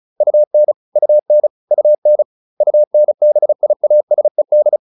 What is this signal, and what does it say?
Morse code, heard as a keyed tone, about 34 wpm, 600 Hz. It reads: UN UN UN UNBIASED